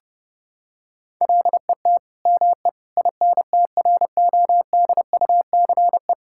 Morse code, heard as a keyed tone, about 30 wpm, 700 Hz. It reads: LET ME INTRODUCE
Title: Sentences from Top 1600 Words 30wpm